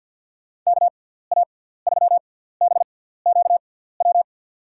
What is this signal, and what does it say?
Morse code, heard as a keyed tone, about 53 wpm, spaced as for 17 wpm, 700 Hz. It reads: KA3BQW
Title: USA Call Signs - 6 Characters - 6x Character Spacing - 50wpm